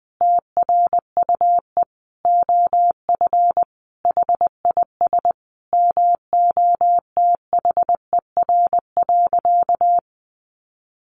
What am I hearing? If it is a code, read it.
Morse code, heard as a keyed tone, about 19 wpm, 700 Hz. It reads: TRUE OF HIS MOTHER.
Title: Sentences from Top 700 Words 20wpm